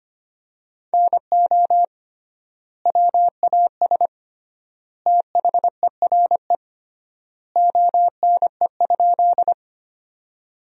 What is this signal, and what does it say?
Morse code, heard as a keyed tone, about 25 wpm, 700 Hz. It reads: NO WAS THERE ONE?
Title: 3x Word-Spacing - Sentences from Top 100 Words - 25wpm